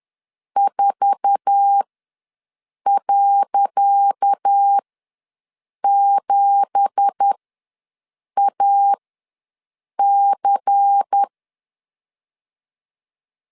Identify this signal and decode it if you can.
Morse code, heard as a keyed tone, about 10 wpm, spaced as for 6 wpm, 800 Hz. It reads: 4.7AC
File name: LEZIONI / 10 SIGNS / 10C DOT ws 10wpm 4213